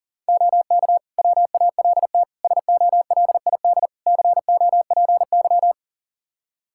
Morse code, heard as a keyed tone, about 40 wpm, 700 Hz. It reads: OK WALT SOLID COPY
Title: Practice QSO - Exchange 3 - More Info - 40wpm